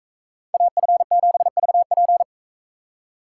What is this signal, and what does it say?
Morse code, heard as a keyed tone, about 38 wpm, 700 Hz. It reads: AF7VP